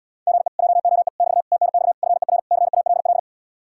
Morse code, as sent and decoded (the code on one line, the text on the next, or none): --. / .--- --- . / -...- / - -. -..- / ..-. .-. / -.-. .- .-.. .-..
GE JOE <BT> TNX FER CALL